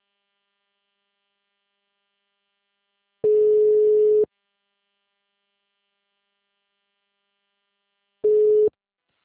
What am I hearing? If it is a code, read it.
Ringback tone (Europe, 425 Hz).